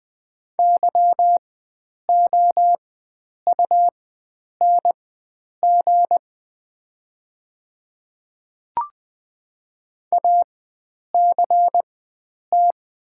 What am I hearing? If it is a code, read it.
Morse code, heard as a keyed tone, about 21 wpm, spaced as for 9 wpm, 700 Hz. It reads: YOUNG ACT
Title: 4x Character Spacing - Top 500 Words - 20wpm